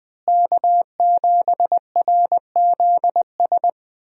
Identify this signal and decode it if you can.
Morse code, heard as a keyed tone, about 20 wpm, 700 Hz. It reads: K7RZS
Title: USA Call Signs - 5 Characters - Repeated 3x - 20wpm